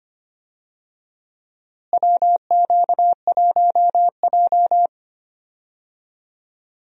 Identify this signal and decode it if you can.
Morse code, heard as a keyed tone, about 25 wpm, 700 Hz. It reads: WQ1J
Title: USA Call Signs - 4 Characters - Repeated 3x - 25wpm